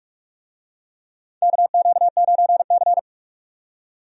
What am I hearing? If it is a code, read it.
Morse code, heard as a keyed tone, about 46 wpm, 700 Hz. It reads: KQ9C
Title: USA Call Signs - 4 Characters - 45wpm